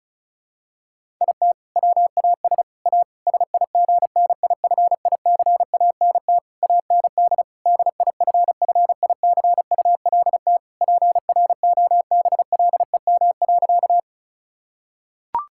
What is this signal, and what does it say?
Morse code, heard as a keyed tone, about 34 wpm, 700 Hz. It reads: IT WAS A SIGNIFICANT AND DIFFICULT PROBLEM.
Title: Sentences from Top 1000 Words - Review Part 6 - 35wpm